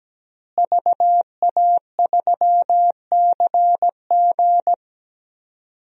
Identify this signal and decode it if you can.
Morse code, heard as a keyed tone, about 17 wpm, 700 Hz. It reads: VA3CG
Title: DXCC-100 Call Signs - 5 Characters - Repeated 2x - 17wpm